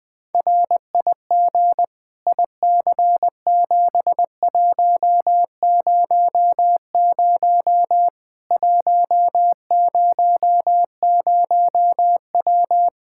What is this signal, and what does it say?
Morse code, heard as a keyed tone, about 20 wpm, 700 Hz. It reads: RIG IC7100 100W
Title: Practice QSO - Exchange 2 - Info - 20wpm